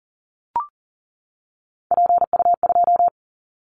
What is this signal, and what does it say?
Morse code, heard as a keyed tone, about 40 wpm, 700 Hz. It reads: PU2
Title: Callsign Prefixes for Top 100 DXCC - 40wpm